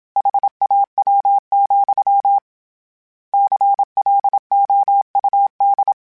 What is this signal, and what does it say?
Morse code, heard as a keyed tone, about 27 wpm, 800 Hz. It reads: HAW, CLOUD
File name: Misc / Misc-mp3s / Word Pairs @25